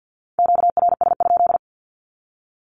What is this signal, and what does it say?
Morse code, heard as a keyed tone, about 50 wpm, 700 Hz. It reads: QRS?
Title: Q-Codes with Spoken Meaning - 50wpm